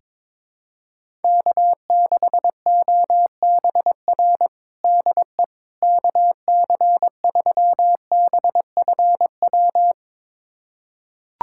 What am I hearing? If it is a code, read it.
Morse code, heard as a keyed tone, about 22 wpm, 700 Hz. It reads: K6OBR DE KC3BFW
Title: USA Callsign DE Callsign 22wpm